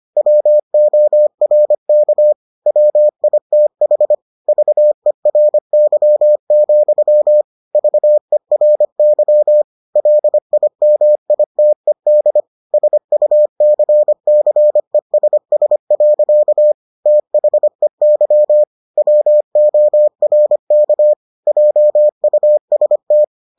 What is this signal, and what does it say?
Morse code, heard as a keyed tone, about 25 wpm, 600 Hz. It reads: WORK WITH VERY, VERY LIMITED SUCCESS. THEY WORK JUST